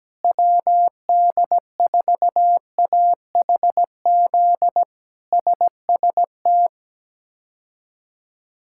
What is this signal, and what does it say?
Morse code, heard as a keyed tone, about 17 wpm, 700 Hz. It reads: WD4AHZ SST
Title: K1USN SST QSO Practice - 17wpm